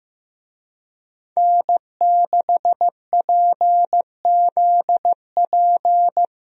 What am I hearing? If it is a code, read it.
Morse code, heard as a keyed tone, about 16 wpm, 700 Hz. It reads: N6PZP
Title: USA Call Signs - 5 Characters - Repeated 2x - 15wpm